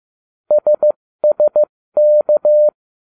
Morse code, heard as a keyed tone, about 15 wpm, spaced as for 13 wpm, 600 Hz. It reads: SSK